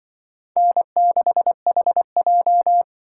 Morse code, heard as a keyed tone, about 23 wpm, 700 Hz. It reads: N6HJ
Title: Speed Racing - USA 1x2 Call Signs 20wpm